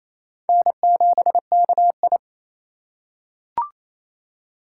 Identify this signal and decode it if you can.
Morse code, heard as a keyed tone, about 28 wpm, 700 Hz. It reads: N7KI
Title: USA Call Signs - 4 Characters - 3x Character Spacing - 28wpm